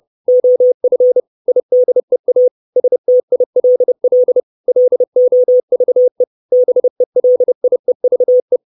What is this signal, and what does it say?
Morse code, heard as a keyed tone, about 30 wpm, 500 Hz. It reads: OF IDEA STILL LOVE BELIEVE
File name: 632 Words 30WPM